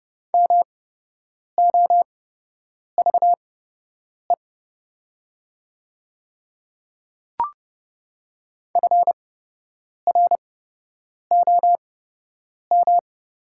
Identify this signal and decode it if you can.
Morse code, heard as a keyed tone, about 30 wpm, spaced as for 8 wpm, 700 Hz. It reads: MOVE FROM
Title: 8x Character Spacing - Top 500 Words - 30wpm